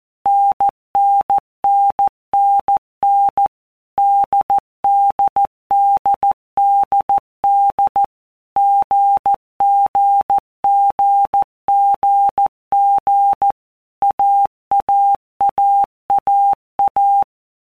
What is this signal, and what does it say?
Morse code, heard as a keyed tone, about 14 wpm, 800 Hz. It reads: NNNNN DDDDD GGGGG AAAAA
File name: Basic Letter Groups By Blocks / Groups@13 / AWUNDG@13